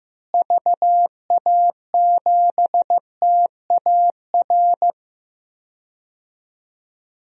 Morse code, heard as a keyed tone, about 15 wpm, 700 Hz. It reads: VA7TAR
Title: DXCC-100 Call Signs - 6 Characters - Repeated 3x - 15wpm